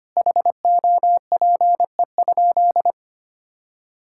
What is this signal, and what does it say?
Morse code, heard as a keyed tone, about 25 wpm, 700 Hz. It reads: HOPE?
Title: Sentences from Top 1000 Words - Review Part 5 - 25wpm